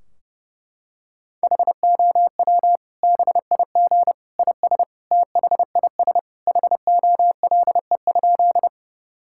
Morse code, heard as a keyed tone, about 31 wpm, 700 Hz. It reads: HOW BIG IS THIS HOLE?
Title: Sentences from Top 1600 Words 30wpm